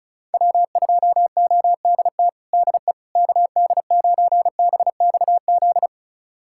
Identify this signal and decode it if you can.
Morse code, heard as a keyed tone, about 34 wpm, 700 Hz. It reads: W2ODT DE KD9BXZ
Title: USA Callsign DE Callsign 35wpm